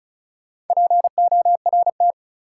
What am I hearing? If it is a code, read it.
Morse code, heard as a keyed tone, about 36 wpm, 700 Hz. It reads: PORT
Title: Sentences from Top 1000 Words - Review Part 1 - New Word Drill - 35wpm